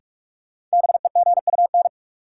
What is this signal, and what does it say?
Morse code, heard as a keyed tone, about 46 wpm, 700 Hz. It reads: BEGUN
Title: Sentences from Top 1000 Words - Review Part 6 - New Word Drill - 45wpm